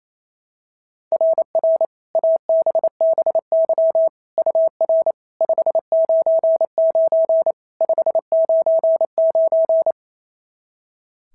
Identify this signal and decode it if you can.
Morse code, heard as a keyed tone, about 28 wpm, 660 Hz. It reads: RR ABBY UR 599 599
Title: SOTA QSO Practice 28wpm